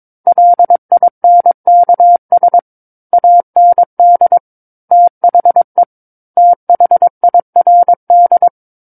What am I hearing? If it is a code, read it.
Morse code, heard as a keyed tone, about 22 wpm, 700 Hz. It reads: LINKS AND THE THIRD
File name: Book One / 13-Chapter 13 - 22 WPM